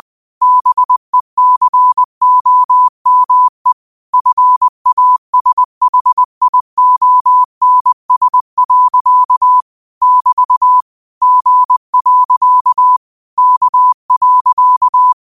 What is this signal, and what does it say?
Morse code, heard as a keyed tone, about 20 wpm, 1000 Hz. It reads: BECOME FASHIONS. <BT> G. K.
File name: qotdfiles / 20wpm